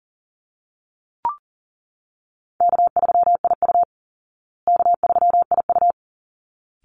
Morse code, heard as a keyed tone, about 40 wpm, 700 Hz. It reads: K3IU K3IU
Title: Easy Call Signs Repeated 2x - 40wpm